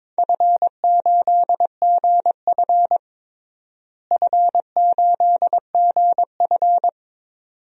Morse code, heard as a keyed tone, about 22 wpm, 700 Hz. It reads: F8GF F8GF
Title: DXCC-100 Call Signs - 4 Characters - Repeated 2x - 22wpm